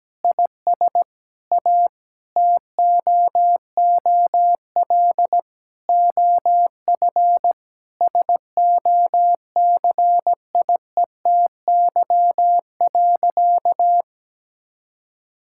Morse code, heard as a keyed tone, about 17 wpm, 700 Hz. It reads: IS A TOOL OF SOCIETY.
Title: Sentences from Top 1000 Words - Review Part 6 - 17wpm